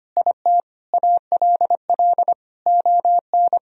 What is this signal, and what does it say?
Morse code, heard as a keyed tone, about 25 wpm, 700 Hz. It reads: IT ALL ON
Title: Sentences from Top 1000 Words - Review Part 2 - 25wpm